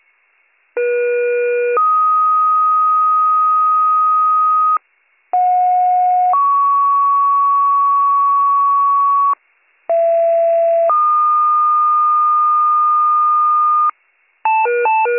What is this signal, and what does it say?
Two-tone page, heard as a one-second tone, then a three-second tone, about 480 then 1250 Hz, about 730 then 1120 Hz, about 670 then 1220 Hz. It is heard from 0.8 to 13.9 s.